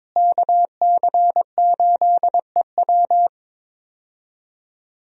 Morse code, heard as a keyed tone, about 22 wpm, 690 Hz. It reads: KC8EW
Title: USA Call Signs - 5 Characters - Repeated 3x - 22wpm